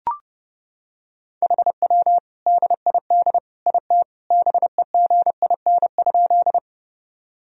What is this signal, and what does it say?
Morse code, heard as a keyed tone, about 30 wpm, 700 Hz. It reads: HW DID IT BEGIN?
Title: Sentences from Top 300 Words and Abbreviations 30wpm